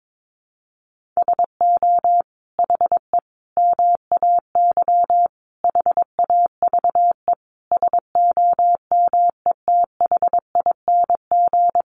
Morse code, heard as a keyed tone, about 22 wpm, 700 Hz. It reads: SO HE MAY HAVE SOMETHING
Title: Sentences from Top 1500 Words 22wpm